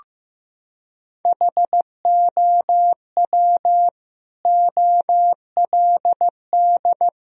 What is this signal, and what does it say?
Morse code, heard as a keyed tone, about 15 wpm, 700 Hz. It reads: HOW OLD
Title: Sentences from Top 1000 Words - Review Part 4 - 15wpm